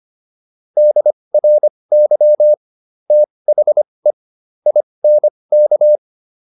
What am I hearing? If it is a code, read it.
Morse code, heard as a keyed tone, about 25 wpm, spaced as for 20 wpm, 600 Hz. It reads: DRY THE INK